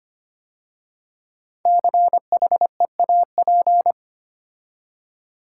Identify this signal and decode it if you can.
Morse code, heard as a keyed tone, about 25 wpm, 700 Hz. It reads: CHEAP
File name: 5 Letter Words - First 3 Letters Given - 25wpm